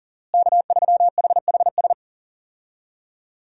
Morse code, heard as a keyed tone, about 40 wpm, 700 Hz. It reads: K3HHS
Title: USA Call Signs - 5 Characters - Repeated 3x - 40wpm